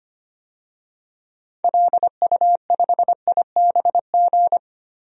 Morse code, heard as a keyed tone, about 25 wpm, 700 Hz. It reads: LU5IBG
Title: DXCC-100 Call Signs - 6 Characters - 25wpm